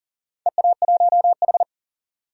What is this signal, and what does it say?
Morse code, heard as a keyed tone, about 40 wpm, 700 Hz. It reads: EA1H
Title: DXCC-100 Call Signs - 4 Characters - 40wpm